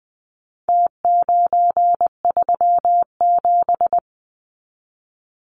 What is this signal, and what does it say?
Morse code, heard as a keyed tone, about 20 wpm, 700 Hz. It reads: T937
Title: Numbers - Contest Style - 4-Digit 20wpm